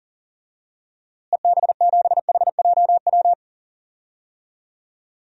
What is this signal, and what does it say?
Morse code, heard as a keyed tone, about 40 wpm, 700 Hz. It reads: EB7HJW
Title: DXCC-100 Call Signs - 6 Characters - Repeated 2x - 40wpm